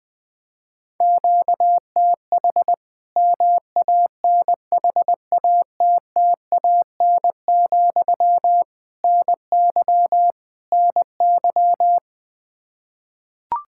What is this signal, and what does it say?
Morse code, heard as a keyed tone, about 20 wpm, 700 Hz. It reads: QTH MANHATTAN, NY NY
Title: Practice QSO - Exchange 1 - Beginning - 20wpm